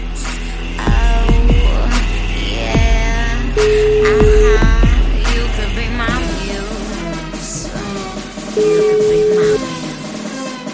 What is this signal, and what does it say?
Ringback tone (Europe, 425 Hz), with noise.